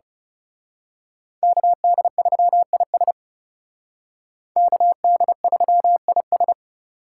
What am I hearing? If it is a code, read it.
Morse code, heard as a keyed tone, about 34 wpm, 700 Hz. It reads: KD3IS KD3IS
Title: Speed Racing - USA 2x2 Call Signs 30wpm